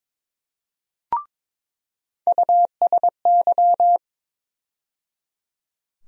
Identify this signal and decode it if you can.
Morse code, heard as a keyed tone, about 22 wpm, 700 Hz. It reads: USY
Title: USA Call Sign Postfixes - 3 Letters 22wpm